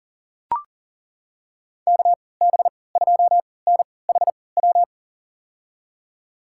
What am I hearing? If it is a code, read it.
Morse code, heard as a keyed tone, about 40 wpm, spaced as for 23 wpm, 700 Hz. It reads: KB2NHW
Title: USA Call Signs - 6 Characters - 3x Character Spacing - 40wpm